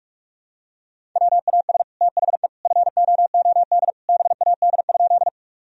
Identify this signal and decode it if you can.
Morse code, heard as a keyed tone, about 41 wpm, 700 Hz. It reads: WAS THE FOOD BAD?
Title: Sentences from Top 1000 Words - Review Part 1 - 45wpm